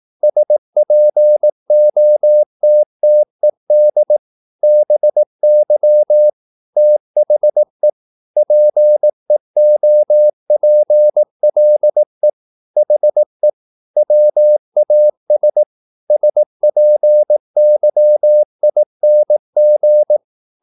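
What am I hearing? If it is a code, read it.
Morse code, heard as a keyed tone, about 18 wpm, 600 Hz. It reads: SPOTTED BY THE PEOPLE HE WAS SPYING
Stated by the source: Morse practice at 18 wpm, English text, from CW med Gnister.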